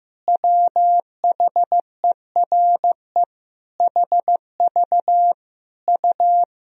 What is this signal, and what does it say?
Morse code, heard as a keyed tone, about 15 wpm, 700 Hz. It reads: WHERE HV U
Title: Sentences from Top 400 Words and Abbreviations 15wpm